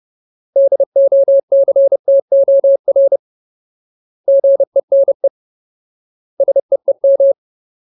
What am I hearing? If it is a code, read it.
Morse code, heard as a keyed tone, about 30 wpm, 550 Hz. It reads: DOCTOR GENE SEEM
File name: Ultimate Word List / 30WPM Ultimate